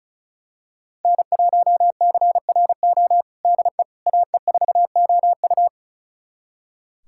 Morse code, heard as a keyed tone, about 36 wpm, 700 Hz. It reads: N1CRO DE AE4OU
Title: USA Callsign DE Callsign - 5 Characters - 35wpm